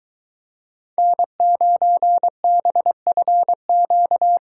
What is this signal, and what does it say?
Morse code, heard as a keyed tone, about 22 wpm, 700 Hz. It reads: N9BFQ